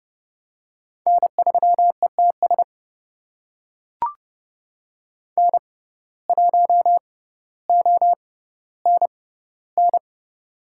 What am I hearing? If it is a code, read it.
Morse code, heard as a keyed tone, about 30 wpm, spaced as for 15 wpm, 700 Hz. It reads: N3ETS N1ONN
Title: USA Call Signs - 5 Characters - 6x Character Spacing - 30wpm